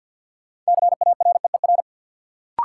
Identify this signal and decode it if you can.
Morse code, heard as a keyed tone, about 50 wpm, 700 Hz. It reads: CAREER
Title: Sentences from Top 1000 Words - New Word Drill - 50wpm